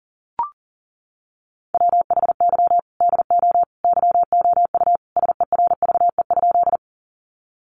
Morse code, heard as a keyed tone, about 40 wpm, 700 Hz. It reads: WHY DO YOU SERVE?
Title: Sentences from Top 400 Words 40wpm